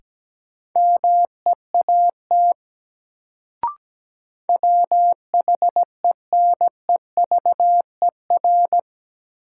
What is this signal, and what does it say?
Morse code, heard as a keyed tone, about 17 wpm, 700 Hz. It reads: MEAT WHENEVER
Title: Sentences from Top 1200 Words - New Word Drill Alt - 17wpm